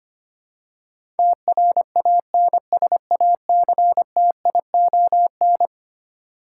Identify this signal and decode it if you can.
Morse code, heard as a keyed tone, about 25 wpm, 700 Hz. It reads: TRANSACTION
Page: morse-code-ninja-practice-25wpm